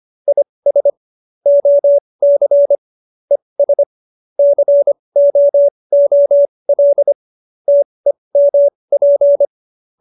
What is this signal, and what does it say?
Morse code, heard as a keyed tone, about 25 wpm, spaced as for 20 wpm, 570 Hz. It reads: IS OC ES COOL TEMP
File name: QSO301 20